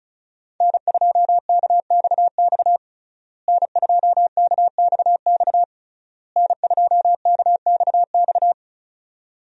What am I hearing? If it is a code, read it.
Morse code, heard as a keyed tone, about 36 wpm, 700 Hz. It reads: N2KXX N2KXX N2KXX
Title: USA Call Signs - 5 Characters - Repeated 3x - 35wpm